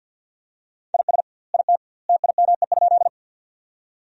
Morse code, heard as a keyed tone, about 51 wpm, 700 Hz. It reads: IS IT TIME?
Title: 2x Word-Spacing - Sentences from Top 100 Words - 50wpm